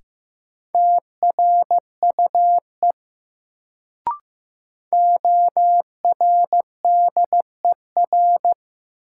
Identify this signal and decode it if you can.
Morse code, heard as a keyed tone, about 15 wpm, 700 Hz. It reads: TRUE ORDER
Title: Sentences from Top 400 Words - New Word Drill - 15wpm